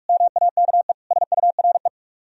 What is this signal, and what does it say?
Morse code, heard as a keyed tone, about 44 wpm, 700 Hz. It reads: MAKE SURE